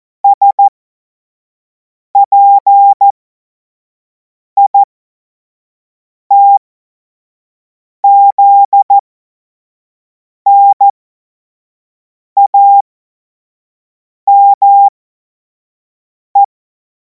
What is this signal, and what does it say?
Morse code, heard as a keyed tone, about 14 wpm, spaced as for 5 wpm, 800 Hz. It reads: SPITZNAME
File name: Morsekurs / Witze / Witz 30. Juni 2026 25 F70